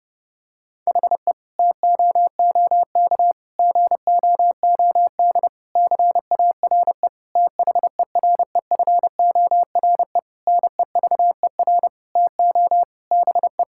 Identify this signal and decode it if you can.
Morse code, heard as a keyed tone, about 30 wpm, 700 Hz. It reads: HE TOOK GOOD CARE THEREFORE NEVER TO BE
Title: Sentences from Top 1000 Words - Review Part 5 - 30wpm